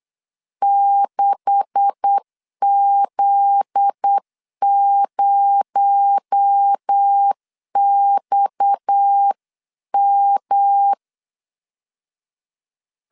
Morse code, heard as a keyed tone, about 8 wpm, 800 Hz. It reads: 6Z0XM
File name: CW / DET / 1 9 1500C S C 8wpm 5653